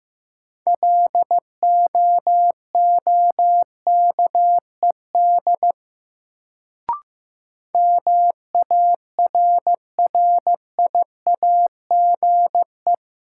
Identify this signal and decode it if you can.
Morse code, heard as a keyed tone, about 15 wpm, 700 Hz. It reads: LOOKED MARRIAGE
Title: Sentences from Top 800 Words - New Word Drill - 15wpm